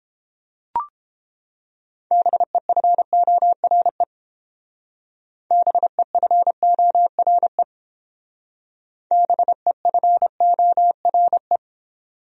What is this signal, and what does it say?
Morse code, heard as a keyed tone, about 31 wpm, 700 Hz. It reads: BEFORE BEFORE BEFORE